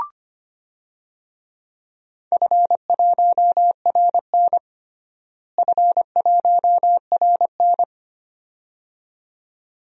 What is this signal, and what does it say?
Morse code, heard as a keyed tone, about 25 wpm, 700 Hz. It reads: F1RN F1RN
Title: DXCC-100 Call Signs - 4 Characters - Repeated 2x - 25wpm